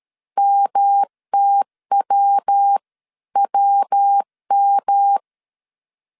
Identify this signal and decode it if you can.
Morse code, heard as a keyed tone, about 12 wpm, 800 Hz. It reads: MTWWM
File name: CW / DET / 1 3 1500C S B 12wpm 2828